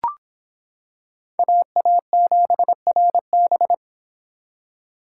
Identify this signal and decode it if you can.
Morse code, heard as a keyed tone, about 26 wpm, 700 Hz. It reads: AA7RB